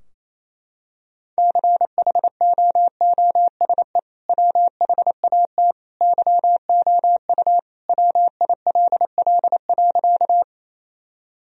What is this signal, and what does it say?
Morse code, heard as a keyed tone, about 28 wpm, 700 Hz. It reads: CHOOSE WHAT YOU WILL.
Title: Sentences from Top 1000 Words - Review Part 4 - 28wpm